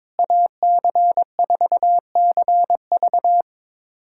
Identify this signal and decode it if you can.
Morse code, heard as a keyed tone, about 22 wpm, 700 Hz. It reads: AC4CV
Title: USA Callsign DE Callsign 22wpm